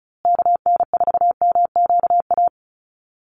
Morse code, heard as a keyed tone, about 34 wpm, 700 Hz. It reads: KN4MQA
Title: USA Call Signs Repeated 3x 35wpm